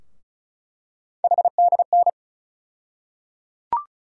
Morse code, heard as a keyed tone, about 34 wpm, 700 Hz. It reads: HDN
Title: USA Call Sign Postfixes - 3 Letters 35wpm